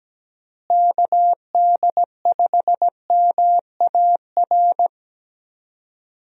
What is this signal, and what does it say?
Morse code, heard as a keyed tone, about 17 wpm, 700 Hz. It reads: KD5MAR
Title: USA Call Signs - 6 Characters - Repeated 3x - 17wpm